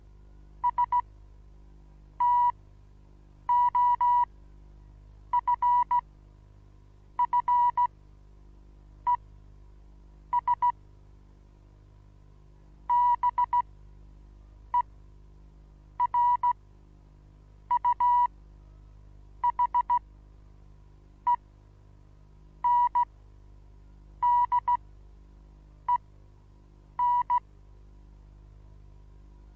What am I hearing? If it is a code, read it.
Morse code, heard as a keyed tone, about 16 wpm, spaced as for 6 wpm, 970 Hz. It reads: STOFFES BERUHENDEN
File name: Onlinelehrgang / morsen / DJ4UF Morsen25